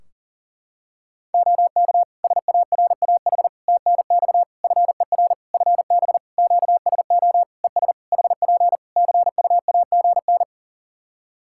Morse code, heard as a keyed tone, about 40 wpm, 700 Hz. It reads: OK SARAH TNX FER FB QSO ES HP CUAGN
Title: Practice QSO - Exchange 4 - Ending - 40wpm